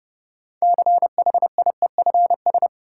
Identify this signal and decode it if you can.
Morse code, heard as a keyed tone, about 30 wpm, 700 Hz. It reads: CHIEFS
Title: Sets of 5 Words Encore - New Word Drill - 30wpm